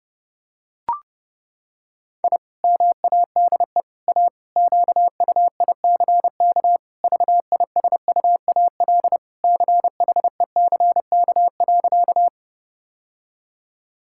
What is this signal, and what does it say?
Morse code, heard as a keyed tone, about 32 wpm, 700 Hz. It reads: I MADE A QUICK VISUAL CHECK.
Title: Sentences from Top 1000 Words - Review Part 6 - 30wpm